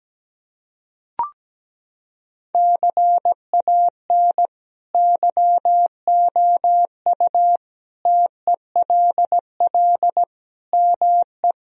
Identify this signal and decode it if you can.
Morse code, heard as a keyed tone, about 17 wpm, 700 Hz. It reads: CAN YOU TELL ME
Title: Sentences from Top 1000 Words - Review Part 3 - 17wpm